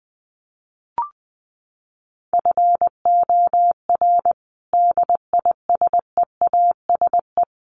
Morse code, heard as a keyed tone, about 20 wpm, 700 Hz. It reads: FOR DISEASE
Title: Sets of 5 Words Encore2 20wpm